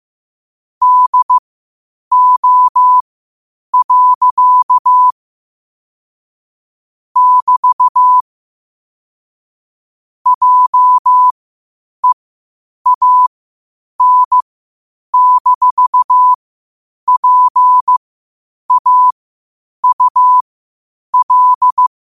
Morse code, heard as a keyed tone, about 15 wpm, spaced as for 8 wpm, 1000 Hz. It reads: DO. <BT> JEAN-PAUL
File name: 8wpm